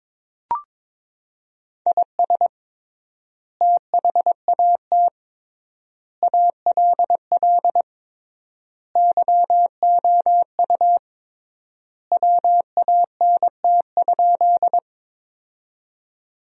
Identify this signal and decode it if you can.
Morse code, heard as a keyed tone, about 22 wpm, 700 Hz. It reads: IS THAT ALL YOU WANT?
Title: 3x Word-Spacing - Sentences from Top 200 Words - 22wpm